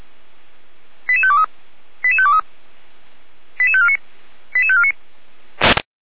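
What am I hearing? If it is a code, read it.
Five-tone selective call (ZVEI), heard as quick runs of five tones, tones of about 70 ms. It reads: 88523 88523 88549 88549
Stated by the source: Five-tone ZVEI selective call received by SDR.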